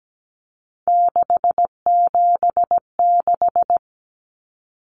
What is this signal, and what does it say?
Morse code, heard as a keyed tone, about 17 wpm, 700 Hz. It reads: 676
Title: Numbers - 3-Digit 17wpm